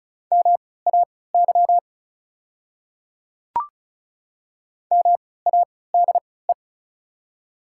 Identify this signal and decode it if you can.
Morse code, heard as a keyed tone, about 36 wpm, spaced as for 20 wpm, 700 Hz. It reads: MAY MADE
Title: 3x Character Spacing - Top 300 Words - 35wpm